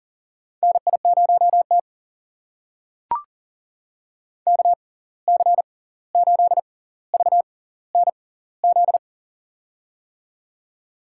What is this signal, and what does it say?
Morse code, heard as a keyed tone, about 40 wpm, spaced as for 18 wpm, 700 Hz. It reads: NI0T KC8VNZ
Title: USA Call Signs - 6x Character Spacing - 40wpm